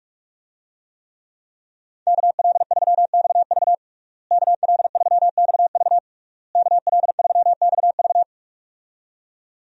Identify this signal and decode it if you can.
Morse code, heard as a keyed tone, about 43 wpm, 700 Hz. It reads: KL3XV KL3XV KL3XV
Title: USA Call Signs - 5 Characters - Repeated 3x - 45wpm